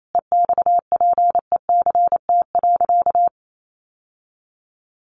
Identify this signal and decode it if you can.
Morse code, heard as a keyed tone, about 28 wpm, 700 Hz. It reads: EXPECT.
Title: Sentences from Top 1100 Words 28wpm